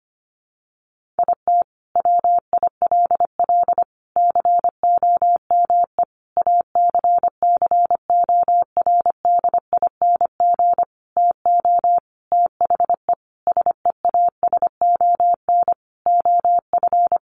Morse code, heard as a keyed tone, about 25 wpm, 700 Hz. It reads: IT WILL COME ACCORDING TO THE SEASON OF
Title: Sentences from Top 1400 Words 25wpm